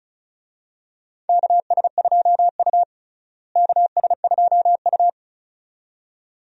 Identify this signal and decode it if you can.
Morse code, heard as a keyed tone, about 34 wpm, 700 Hz. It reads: KS2U KS2U
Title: USA Call Signs - 4 Characters - Repeated 2x - 35wpm